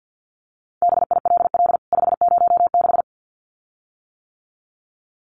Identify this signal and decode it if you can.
Morse code, heard as a keyed tone, about 50 wpm, 700 Hz. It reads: BILL 506
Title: CWT Exchange 50wpm